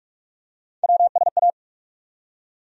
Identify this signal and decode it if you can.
Morse code, heard as a keyed tone, about 45 wpm, 700 Hz. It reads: WSA
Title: USA Call Sign Postfixes - 3 Letters 45wpm